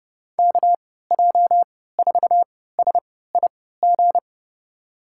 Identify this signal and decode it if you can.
Morse code, heard as a keyed tone, about 30 wpm, spaced as for 17 wpm, 700 Hz. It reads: KJ4SIG
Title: USA Call Signs - 6 Characters - 3x Character Spacing - 30wpm